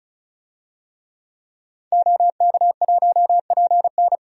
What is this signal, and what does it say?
Morse code, heard as a keyed tone, about 34 wpm, 700 Hz. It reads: OK1PN
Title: DXCC-100 Call Signs - 5 Characters - 35wpm